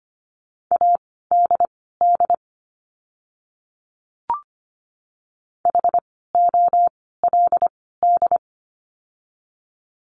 Morse code, heard as a keyed tone, about 25 wpm, spaced as for 16 wpm, 700 Hz. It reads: ADD HOLD
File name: course / 25:10wpm / 094 Lesson 16f - Letter c - Words 25wpm